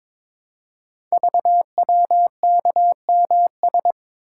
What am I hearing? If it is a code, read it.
Morse code, heard as a keyed tone, about 22 wpm, 700 Hz. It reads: VWKMS
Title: Code Groups - 5 Letters - 22wpm